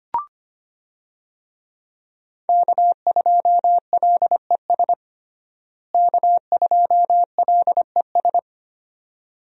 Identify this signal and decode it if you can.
Morse code, heard as a keyed tone, about 25 wpm, 700 Hz. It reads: K2LES K2LES
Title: USA Call Signs - 5 Characters - Repeated 2x - 25wpm